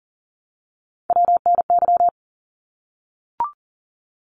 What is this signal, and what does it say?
Morse code, heard as a keyed tone, about 40 wpm, 700 Hz. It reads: WNY
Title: ARRL Sections 40wpm